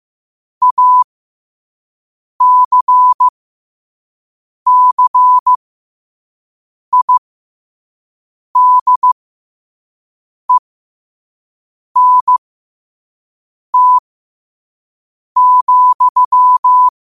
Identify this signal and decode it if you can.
Morse code, heard as a keyed tone, about 15 wpm, spaced as for 5 wpm, 1000 Hz. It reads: ACCIDENT,
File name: qotdfiles / 5wpm